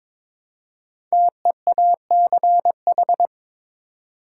Morse code, heard as a keyed tone, about 22 wpm, 700 Hz. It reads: TEACH
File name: 5 Letter Words - First 3 Letters Given - 22wpm